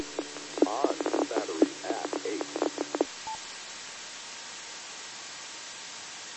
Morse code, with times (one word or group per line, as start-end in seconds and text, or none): none